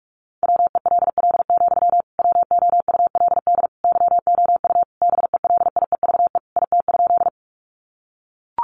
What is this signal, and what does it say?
Morse code, heard as a keyed tone, about 44 wpm, 700 Hz. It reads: WELL, WOULD YOU BELIEVE IT?
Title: Sentences from Top 1000 Words 45wpm